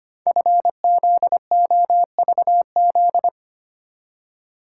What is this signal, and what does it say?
Morse code, heard as a keyed tone, about 25 wpm, 700 Hz. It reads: FZOVZ
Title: Code Groups - 5 Letters - 25wpm